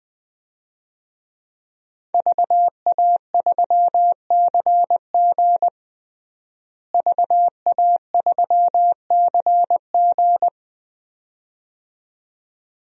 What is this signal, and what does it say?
Morse code, heard as a keyed tone, about 20 wpm, 700 Hz. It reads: VA3CG VA3CG
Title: DXCC-100 Call Signs - 5 Characters - Repeated 2x - 20wpm